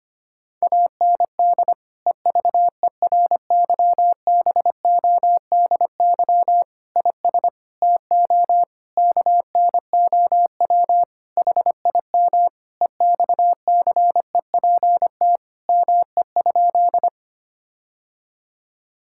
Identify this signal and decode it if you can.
Morse code, heard as a keyed tone, about 25 wpm, 700 Hz. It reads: AND EVERYBODY IS TO KNOW HIM EXCEPT ME?
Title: Sentences from Top 1600 Words 25wpm